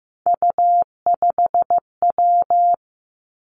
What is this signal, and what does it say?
Morse code, heard as a keyed tone, about 15 wpm, 700 Hz. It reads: U5W
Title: Three Challenging Letter-Number Combinations 15wpm